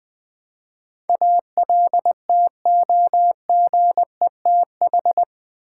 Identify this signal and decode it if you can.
Morse code, heard as a keyed tone, about 20 wpm, 700 Hz. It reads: ALTOGETH
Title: Build A Word - 4 Letters At A Time - 10-Letter Words - 20wpm